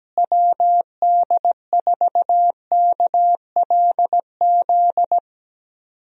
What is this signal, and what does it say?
Morse code, heard as a keyed tone, about 17 wpm, 700 Hz. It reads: WD4KLZ
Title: USA Call Signs - 6 Characters - Repeated 3x - 17wpm